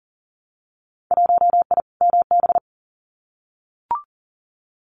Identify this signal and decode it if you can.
Morse code, heard as a keyed tone, about 38 wpm, 700 Hz. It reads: 1I MB
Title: Winter Field Day Exchanges Encore 40wpm